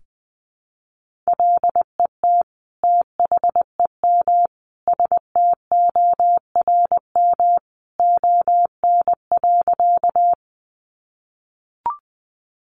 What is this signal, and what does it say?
Morse code, heard as a keyed tone, about 20 wpm, 700 Hz. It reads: LET THEM STORM ON.
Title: Sentences from Top 1500 Words 20wpm